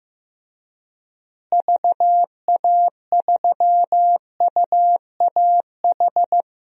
Morse code, heard as a keyed tone, about 15 wpm, 700 Hz. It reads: VA3UAH
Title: DXCC-100 Call Signs - 6 Characters - Repeated 3x - 15wpm